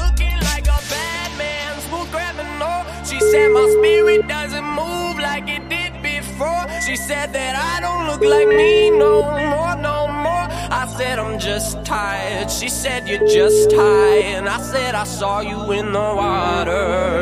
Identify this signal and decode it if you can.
Ringback tone (Europe, 425 Hz), with noise.